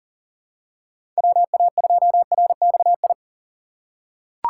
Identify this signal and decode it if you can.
Morse code, heard as a keyed tone, about 40 wpm, 700 Hz. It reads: WA2RXI
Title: USA Call Signs - 6 Characters - 6x Character Spacing - 40wpm